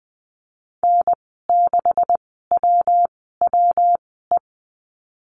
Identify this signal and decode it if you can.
Morse code, heard as a keyed tone, about 19 wpm, spaced as for 14 wpm, 700 Hz. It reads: N6WWE
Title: 193 Lesson 31g - Letter v - Call Signs 20wpm